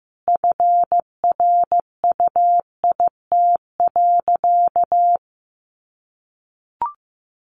Morse code, heard as a keyed tone, about 15 wpm, 700 Hz. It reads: FRUIT.
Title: Sentences from Top 1300 Words 15wpm